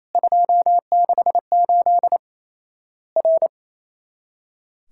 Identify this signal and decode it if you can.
Morse code, heard as a keyed tone, about 27 wpm, 700 Hz. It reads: 268 R
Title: Worked All Europe DX Contest - QTC Exchange - 28wpm